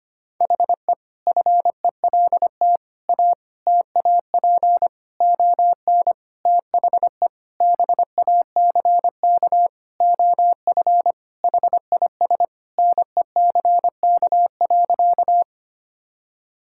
Morse code, heard as a keyed tone, about 25 wpm, 700 Hz. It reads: HE FELT A TAP ON THE BACK OF HIS NECK.
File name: practice / 25wpm / Sentences with 4-Word Phrases - 25wpm